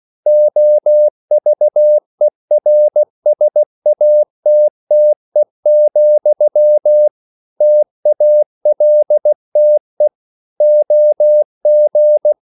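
Morse code, heard as a keyed tone, about 16 wpm, 600 Hz. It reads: OVERSATTE, TALTE OG
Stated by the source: Slow (QRS) Morse code at 16 wpm, from CW med Gnister.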